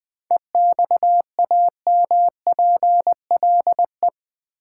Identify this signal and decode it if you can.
Morse code, heard as a keyed tone, about 20 wpm, 700 Hz. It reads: EXAMPLE
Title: Sets of 2 Words - New Word Drill - 20wpm